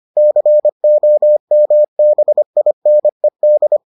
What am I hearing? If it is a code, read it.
Morse code, heard as a keyed tone, about 25 wpm, 600 Hz. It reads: COMBINED